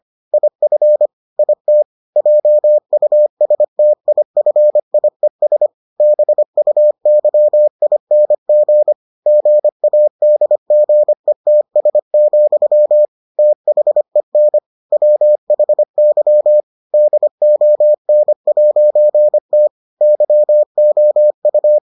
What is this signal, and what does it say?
Morse code, heard as a keyed tone, about 25 wpm, 600 Hz. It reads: IF IT JUSTIFIES BUYING GADGETS, THEN WHY DON'T YOU